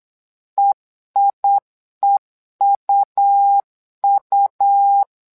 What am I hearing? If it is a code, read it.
Morse code, heard as a keyed tone, about 8 wpm, 800 Hz. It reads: EIEUU
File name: LEZIONI / 2 UHDN / 2 UHDN 8wpm2232